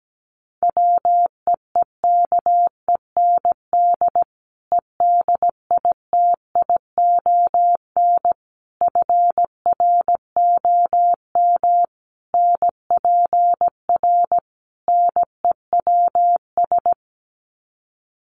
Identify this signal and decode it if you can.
Morse code, heard as a keyed tone, about 17 wpm, 700 Hz. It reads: WEEKEND EDITION FROM NPR NEWS
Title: Sets of 5 Words 17wpm